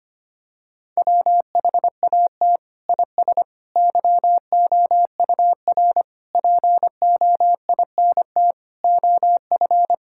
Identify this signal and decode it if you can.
Morse code, heard as a keyed tone, about 25 wpm, 700 Hz. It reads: WHAT IS YOUR POINT OF
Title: Sentences from Top 1000 Words - Review Part 6 - 25wpm